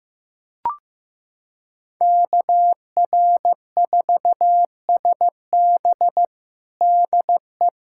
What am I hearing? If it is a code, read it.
Morse code, heard as a keyed tone, about 15 wpm, 700 Hz. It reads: KR4SB DE
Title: USA Callsign DE Callsign 15wpm